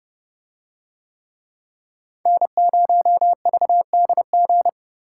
Morse code, heard as a keyed tone, about 28 wpm, 700 Hz. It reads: N0VDG